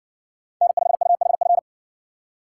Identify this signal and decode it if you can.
Morse code, heard as a keyed tone, about 57 wpm, 700 Hz. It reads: N5UHF